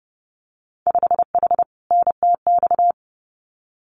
Morse code, heard as a keyed tone, about 30 wpm, 700 Hz. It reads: 5H NTX
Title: Winter Field Day Exchanges Encore 30wpm